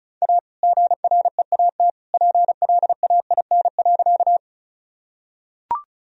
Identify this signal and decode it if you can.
Morse code, heard as a keyed tone, about 36 wpm, 700 Hz. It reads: A GREAT PLAIN.
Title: Sentences from Top 1000 Words - Review Part 2 - 35wpm